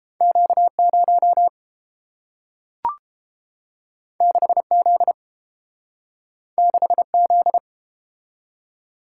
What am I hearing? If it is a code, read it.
Morse code, heard as a keyed tone, about 31 wpm, 700 Hz. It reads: Q0 6Z 6Z